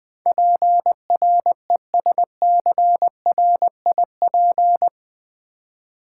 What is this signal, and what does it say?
Morse code, heard as a keyed tone, about 20 wpm, 700 Hz. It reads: PRESCRIP
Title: Build A Word - 4 Letters At A Time - 12-Letter Words - 20wpm